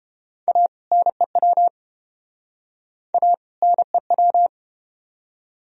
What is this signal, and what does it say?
Morse code, heard as a keyed tone, about 32 wpm, 700 Hz. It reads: A NEW A NEW